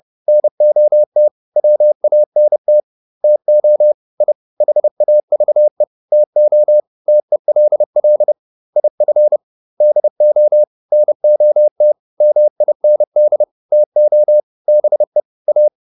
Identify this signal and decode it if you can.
Morse code, heard as a keyed tone, about 30 wpm, 600 Hz. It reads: NOT WANT TO I HAVE TO TELL IF DO NOT MIND TO BE A